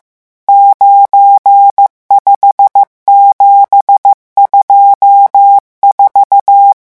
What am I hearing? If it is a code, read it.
Morse code, heard as a keyed tone, about 15 wpm, 800 Hz. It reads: 95724
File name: morse mp3 / revision11 chiffres vitesse 14